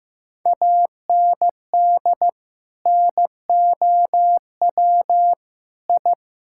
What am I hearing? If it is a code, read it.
Morse code, heard as a keyed tone, about 16 wpm, 700 Hz. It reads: AND NOW I
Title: Sentences from Top 1000 Words - Review Part 6 - 15wpm